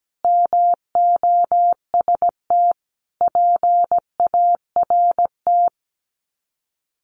Morse code, heard as a keyed tone, about 17 wpm, 700 Hz. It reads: MOST PART
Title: Sets of 4 Words 17wpm